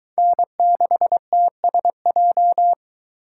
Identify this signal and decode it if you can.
Morse code, heard as a keyed tone, about 22 wpm, 700 Hz. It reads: N6TSJ